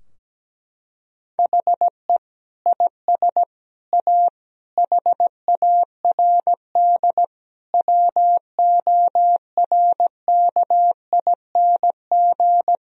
Morse code, heard as a keyed tone, about 17 wpm, 700 Hz. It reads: HE IS A HARD WORKING
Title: Sentences from Top 1000 Words - Review Part 4 - 17wpm